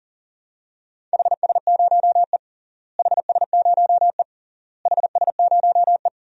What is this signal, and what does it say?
Morse code, heard as a keyed tone, about 40 wpm, 700 Hz. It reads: HS0E HS0E HS0E
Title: DXCC-100 Call Signs - 4 Characters - Repeated 3x - 40wpm